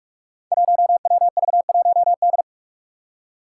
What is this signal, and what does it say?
Morse code, heard as a keyed tone, about 43 wpm, 700 Hz. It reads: 1WV1D
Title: Five Challenging Letter-Number Combinations 45wpm